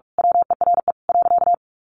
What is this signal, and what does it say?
Morse code, heard as a keyed tone, about 44 wpm, 700 Hz. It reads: WERE .
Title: A Sound Of Thunder 45wpm